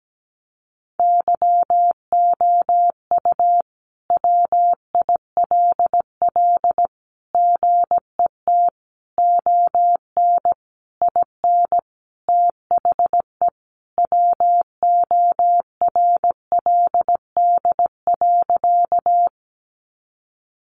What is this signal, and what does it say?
Morse code, heard as a keyed tone, about 17 wpm, 700 Hz. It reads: YOU WILL GET ON IN THE WORLD.
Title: Sentences from Top 200 Words 17wpm